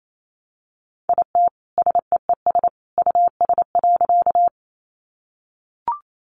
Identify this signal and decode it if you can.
Morse code, heard as a keyed tone, about 28 wpm, 700 Hz. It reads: IT SEES US.
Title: A Sound Of Thunder 28wpm